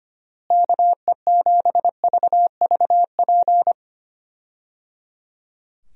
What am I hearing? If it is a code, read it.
Morse code, heard as a keyed tone, about 25 wpm, 700 Hz. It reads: KE7VVP
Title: USA Call Signs - 6 Characters - Repeated 3x - 25wpm